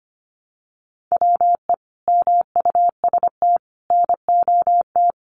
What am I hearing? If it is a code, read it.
Morse code, heard as a keyed tone, about 25 wpm, 700 Hz. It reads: WE MUST NOT
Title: Sentences from Top 1100 Words 25wpm